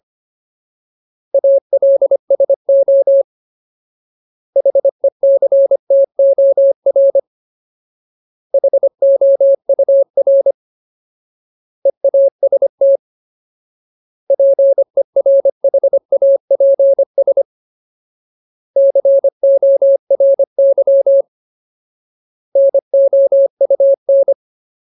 Morse code, heard as a keyed tone, about 25 wpm, 550 Hz. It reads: ALSO HECTOR HOUR EAST PERHAPS CORY NOUN
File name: Ultimate Word List / 25WPM Ultimate